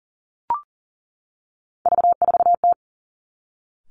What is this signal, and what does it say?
Morse code, heard as a keyed tone, about 40 wpm, 700 Hz. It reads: V4T
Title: Three Challenging Letter-Number Combinations 40wpm